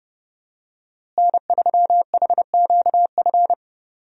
Morse code, heard as a keyed tone, about 30 wpm, 700 Hz. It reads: N3HQF
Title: USA Call Signs - 5 Characters - Repeated 2x - 30wpm